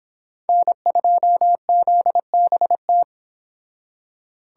Morse code, heard as a keyed tone, about 26 wpm, 700 Hz. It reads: N2ZBT
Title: Speed Racing - USA 1x3 Call Signs 22wpm